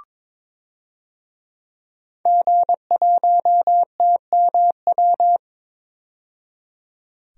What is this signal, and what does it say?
Morse code, heard as a keyed tone, about 21 wpm, 700 Hz. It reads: G1TMW
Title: DXCC-100 Call Signs - 5 Characters - 22wpm